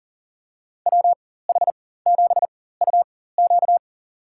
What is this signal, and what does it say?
Morse code, heard as a keyed tone, about 38 wpm, spaced as for 18 wpm, 700 Hz. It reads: WH7UQ
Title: USA Call Signs - 5 Characters - 4x Character Spacing - 40wpm